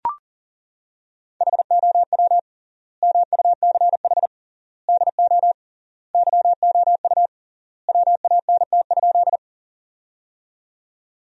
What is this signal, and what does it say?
Morse code, heard as a keyed tone, about 38 wpm, 700 Hz. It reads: HOW MUCH DO YOU WANT?
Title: 3x Word-Spacing - Sentences from Top 200 Words - 40wpm